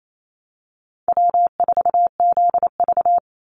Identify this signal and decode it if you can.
Morse code, heard as a keyed tone, about 28 wpm, 700 Hz. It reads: W4ZV
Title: NA SOTA Chaser Callsigns 28wpm